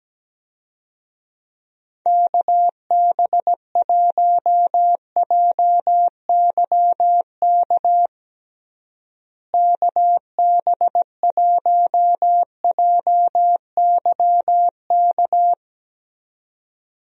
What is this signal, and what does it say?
Morse code, heard as a keyed tone, about 17 wpm, 700 Hz. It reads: KB1JYK KB1JYK
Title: USA Call Signs - 6 Characters - Repeated 2x - 17wpm